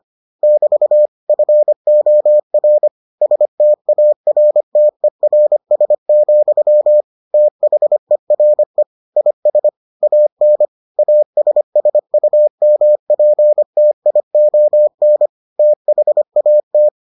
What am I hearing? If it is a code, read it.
Morse code, heard as a keyed tone, about 25 wpm, 600 Hz. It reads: <BT> FOR STARTERS, THERE IS AN ASSUMPTION THAT